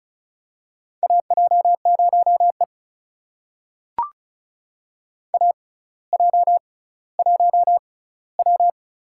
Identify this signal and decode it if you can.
Morse code, heard as a keyed tone, about 36 wpm, spaced as for 18 wpm, 700 Hz. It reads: AJ0E AJ1W
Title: USA Call Signs - 4 Characters - 6x Character Spacing - 35wpm